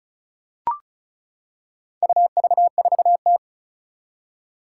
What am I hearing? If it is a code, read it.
Morse code, heard as a keyed tone, about 36 wpm, 700 Hz. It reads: UV4T
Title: Four Challenging Letter-Number Combinations 35wpm